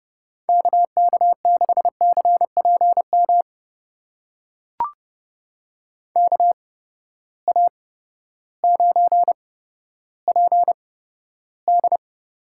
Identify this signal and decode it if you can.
Morse code, heard as a keyed tone, about 30 wpm, spaced as for 14 wpm, 700 Hz. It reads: KK6CPM KA9PD
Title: USA Call Signs - 6 Characters - 8x Character Spacing - 30wpm